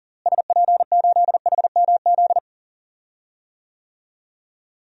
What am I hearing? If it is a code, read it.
Morse code, heard as a keyed tone, about 40 wpm, 700 Hz. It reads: SP8HMZ
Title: DXCC-100 Call Signs - 6 Characters - Repeated 3x - 40wpm